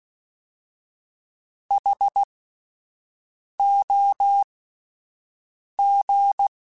Morse code, heard as a keyed tone, about 16 wpm, spaced as for 5 wpm, 800 Hz. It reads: HOG
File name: Alphabetical / Alphabetical mp3s / Hs @5